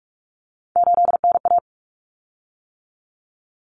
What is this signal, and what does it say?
Morse code, heard as a keyed tone, about 44 wpm, 700 Hz. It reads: 8NA
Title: Common Cut Numbers - 3-Digit 45wpm